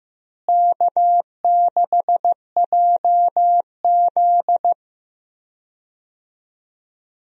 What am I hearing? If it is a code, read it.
Morse code, heard as a keyed tone, about 15 wpm, 700 Hz. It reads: K6JZ
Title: USA Call Signs - 4 Characters - Repeated 2x - 15wpm